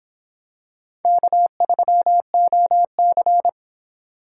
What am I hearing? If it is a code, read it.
Morse code, heard as a keyed tone, about 25 wpm, 700 Hz. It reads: K3OC